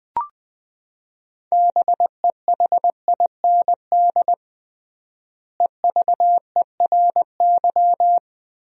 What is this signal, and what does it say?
Morse code, heard as a keyed tone, about 20 wpm, 700 Hz. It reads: BEHIND EVERY
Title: 3x Word-Spacing - Sentences from Top 500 Words - 20wpm